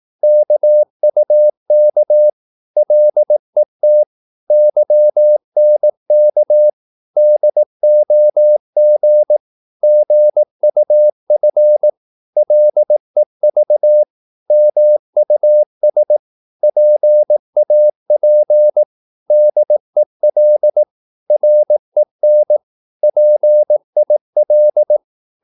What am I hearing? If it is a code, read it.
Morse code, heard as a keyed tone, about 18 wpm, 600 Hz. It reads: KUK LET YNK DOG GUF LEV MUS PAP DEL REN PIL
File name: Korte ord DK 18wpm